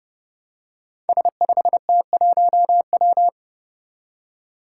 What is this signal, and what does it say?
Morse code, heard as a keyed tone, about 32 wpm, 700 Hz. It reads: S5T1W
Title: Five Challenging Letter-Number Combinations 30wpm